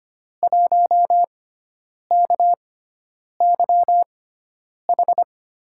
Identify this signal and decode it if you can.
Morse code, heard as a keyed tone, about 25 wpm, spaced as for 9 wpm, 700 Hz. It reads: 1KYH
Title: USA Call Signs - 6 Characters - 6x Character Spacing - 25wpm